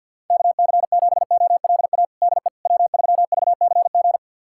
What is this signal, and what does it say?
Morse code, heard as a keyed tone, about 48 wpm, 700 Hz. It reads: KC7OLA DE W3VCG
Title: USA Callsign DE Callsign 50wpm